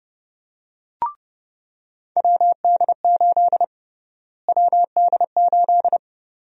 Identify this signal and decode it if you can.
Morse code, heard as a keyed tone, about 31 wpm, 700 Hz. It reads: WD8 WD8
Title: USA Call Sign Prefixes - Repeated 2x - 30wpm